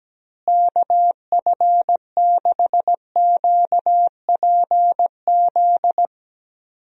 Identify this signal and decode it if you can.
Morse code, heard as a keyed tone, about 17 wpm, 700 Hz. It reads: KF6QPZ
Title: USA Call Signs - 6 Characters - Repeated 3x - 17wpm